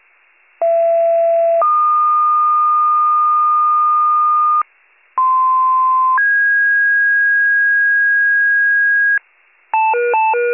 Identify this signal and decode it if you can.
Two-tone page, heard as a one-second tone, then a three-second tone, about 670 then 1220 Hz, about 1040 then 1650 Hz, with noise. It is heard from 0.6 to 9.2 s.